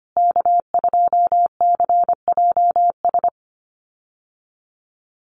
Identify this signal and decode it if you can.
Morse code, heard as a keyed tone, about 25 wpm, 700 Hz. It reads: K2CJS
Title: USA Call Signs Repeated 3x 25wpm